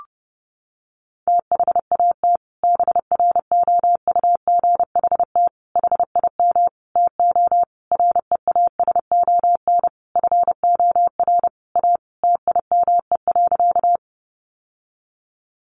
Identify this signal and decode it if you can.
Morse code, heard as a keyed tone, about 28 wpm, 700 Hz. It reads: THAT BROUGHT HIM TO REASON FOR A TIME.
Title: Sentences from Top 700 Words 30wpm